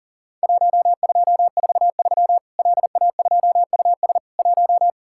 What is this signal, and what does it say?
Morse code, heard as a keyed tone, about 40 wpm, 700 Hz. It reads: 1243 LA2US 1
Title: Worked All Europe DX Contest - QTC Exchange - 40wpm